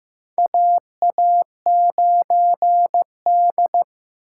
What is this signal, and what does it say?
Morse code, heard as a keyed tone, about 15 wpm, 700 Hz. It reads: AA9D
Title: USA Call Signs - 4 Characters - 4x Character Spacing - 15wpm